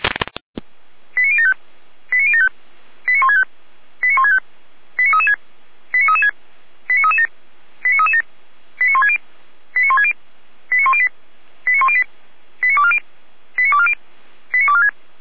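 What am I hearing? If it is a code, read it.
Five-tone selective call (ZVEI), heard as quick runs of five tones, tones of about 80 ms. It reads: 89075 89075 89165 89165 89336 89336 89337 89337 89160 89160 89108 89108 89240 89240 89246
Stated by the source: ZVEI five-tone selective call 89170, received by SDR.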